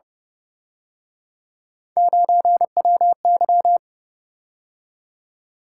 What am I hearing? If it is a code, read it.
Morse code, heard as a keyed tone, about 31 wpm, 700 Hz. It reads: 9WY